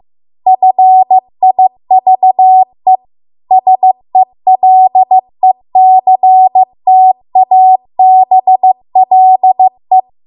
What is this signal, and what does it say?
Morse code, heard as a keyed tone, about 15 wpm, 750 Hz. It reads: FIVE SELECTABLE
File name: Archive / 15 WPM / 15WPM